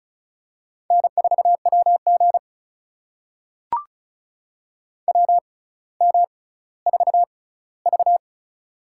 Morse code, heard as a keyed tone, about 36 wpm, spaced as for 18 wpm, 700 Hz. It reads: N4WG WM4V
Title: USA Call Signs - 4 Characters - 6x Character Spacing - 35wpm